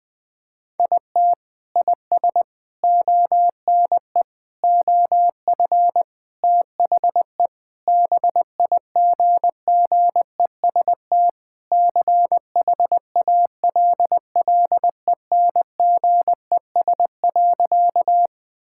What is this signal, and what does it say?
Morse code, heard as a keyed tone, about 20 wpm, 700 Hz. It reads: IT IS ONE OF THE BIGGEST CHALLENGES.
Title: Sentences with 4-Word Phrases - Spoken Beginning - 20wpm